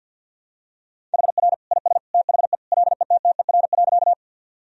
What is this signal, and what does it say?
Morse code, heard as a keyed tone, about 52 wpm, 700 Hz. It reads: HR IS THE LETTER.
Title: Sentences from Top 300 Words and Abbreviations 50wpm